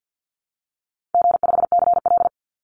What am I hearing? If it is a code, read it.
Morse code, heard as a keyed tone, about 50 wpm, 700 Hz. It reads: G5CL
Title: Easy Call Signs Repeated 2x - 50wpm